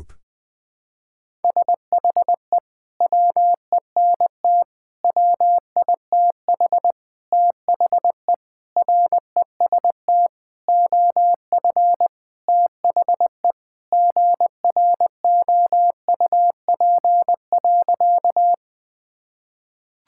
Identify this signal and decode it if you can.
Morse code, heard as a keyed tone, about 20 wpm, 700 Hz. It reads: SHE WENT WITH THE REST OF THE GROUP.
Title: Sentences with 5-Word Phrases - 20wpm